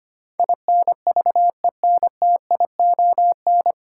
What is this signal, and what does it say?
Morse code, heard as a keyed tone, about 25 wpm, 700 Hz. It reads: INVENTION
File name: Nine Letter Words - 25wpm